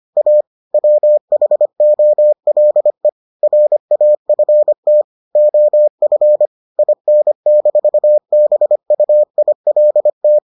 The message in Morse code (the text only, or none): A WHOLE RAFT OF IN-BUILT